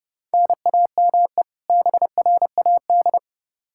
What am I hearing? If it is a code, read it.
Morse code, heard as a keyed tone, about 30 wpm, 700 Hz. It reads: NAME BRAD
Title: Practice QSO - Exchange 1 - Beginning - 30wpm